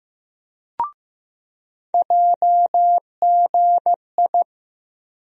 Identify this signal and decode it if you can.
Morse code, heard as a keyed tone, about 15 wpm, 700 Hz. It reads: JGI
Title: USA Call Sign Postfixes - 3 Letters 15wpm